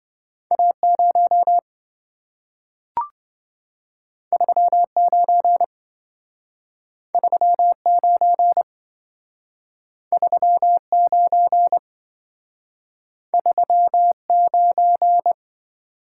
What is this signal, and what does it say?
Morse code, heard as a keyed tone, about 27 wpm, 700 Hz. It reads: A0 39 39 39 39